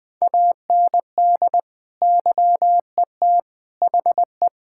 Morse code, heard as a keyed tone, about 21 wpm, 700 Hz. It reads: AND YET HE
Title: Sentences from Top 1600 Words 20wpm